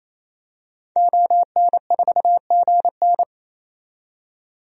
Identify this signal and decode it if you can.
Morse code, heard as a keyed tone, about 28 wpm, 700 Hz. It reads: ON4GN
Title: DXCC-100 Call Signs - 5 Characters - 28wpm